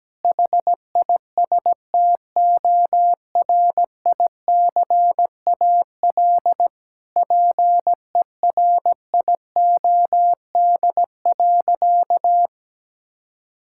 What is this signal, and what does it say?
Morse code, heard as a keyed tone, about 17 wpm, 700 Hz. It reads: HISTORICAL PERIOD.
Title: Sentences from Top 1000 Words - Review Part 5 - 17wpm